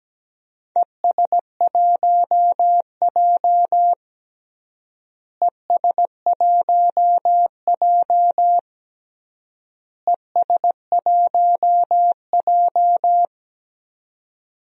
Morse code, heard as a keyed tone, about 17 wpm, 700 Hz. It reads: ES1J ES1J ES1J
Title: DXCC-100 Call Signs - 4 Characters - Repeated 3x - 17wpm